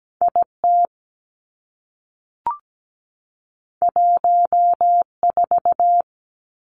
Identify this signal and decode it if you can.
Morse code, heard as a keyed tone, about 17 wpm, 700 Hz. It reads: I T 1 4
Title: Two Challenging Letter-Number Combinations 17wpm